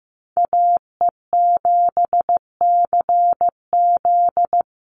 Morse code, heard as a keyed tone, about 15 wpm, 700 Hz. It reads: AE7CZ
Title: USA Call Signs Repeated 3x 15wpm